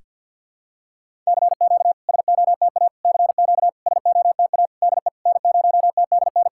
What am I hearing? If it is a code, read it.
Morse code, heard as a keyed tone, about 50 wpm, 690 Hz. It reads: CQ SOTA CQ SOTA DE N0TDN
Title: SOTA QSO Practice 50wpm